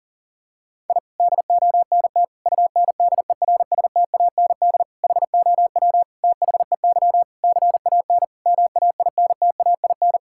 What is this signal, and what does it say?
Morse code, heard as a keyed tone, about 37 wpm, 700 Hz. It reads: I DONT UNDERSTAND HOW THEY CAN MAINTAIN
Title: Sentences from Top 1000 Words - Review Part 5 - 40wpm